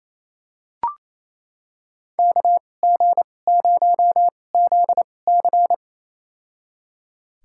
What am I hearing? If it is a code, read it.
Morse code, heard as a keyed tone, about 28 wpm, spaced as for 20 wpm, 700 Hz. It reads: KG0ZC
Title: USA Call Signs - 5 Characters - 2x Character Spacing - 28wpm